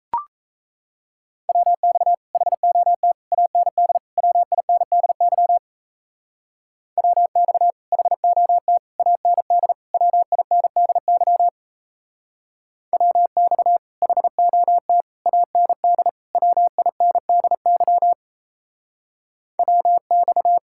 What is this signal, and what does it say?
Morse code, heard as a keyed tone, about 33 wpm, 700 Hz. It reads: WX HOT AND WINDY WX HOT AND WINDY WX HOT AND WINDY WX
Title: Speed-Racing Common QSO Phrases - 28wpm